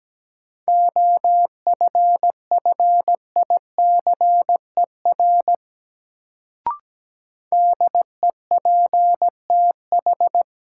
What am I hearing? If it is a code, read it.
Morse code, heard as a keyed tone, about 17 wpm, 700 Hz. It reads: OFFICER DEPTH
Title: Sentences from Top 1200 Words - New Word Drill Alt - 17wpm